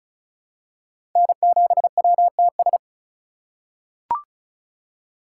Morse code, heard as a keyed tone, about 36 wpm, 700 Hz. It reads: N7WTS
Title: USA Call Signs - 4x Character Spacing - 35wpm